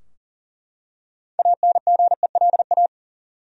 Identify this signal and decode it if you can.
Morse code, heard as a keyed tone, about 40 wpm, 700 Hz. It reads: ANGELA
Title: USA Common Names - 40wpm